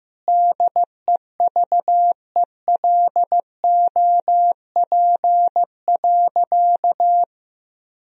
Morse code, heard as a keyed tone, about 16 wpm, 700 Hz. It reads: DEVELOP.
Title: Sentences from Top 1000 Words - Review Part 2 - 15wpm